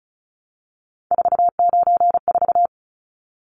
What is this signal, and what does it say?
Morse code, heard as a keyed tone, about 34 wpm, 700 Hz. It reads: 494
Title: Numbers - 3-Digit 35wpm